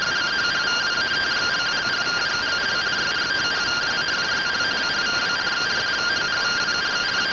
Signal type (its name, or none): two-tone teleprinter signal